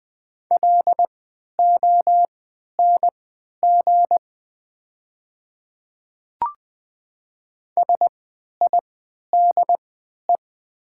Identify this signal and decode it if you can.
Morse code, heard as a keyed tone, about 20 wpm, spaced as for 11 wpm, 700 Hz. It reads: LONG SIDE
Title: 3x Character Spacing - Top 100 Words - 20wpm